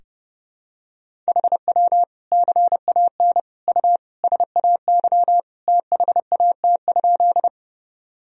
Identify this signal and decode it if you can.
Morse code, heard as a keyed tone, about 30 wpm, 700 Hz. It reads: HW CAN U SAY THAT?
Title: Sentences from Top 200 Words and Abbreviations 30wpm